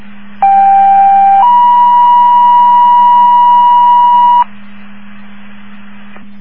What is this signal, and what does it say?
Two-tone page, heard as a one-second tone, then a three-second tone, about 790 then 970 Hz. It is heard from 0.4 to 4.4 s.
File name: Woodgate